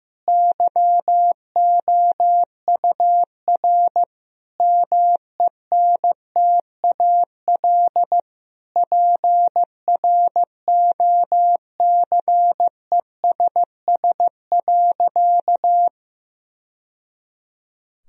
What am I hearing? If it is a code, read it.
Morse code, heard as a keyed tone, about 15 wpm, 700 Hz. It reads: YOUR MENTAL PROCESS.
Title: Sentences from Top 1000 Words - Review Part 4 - 15wpm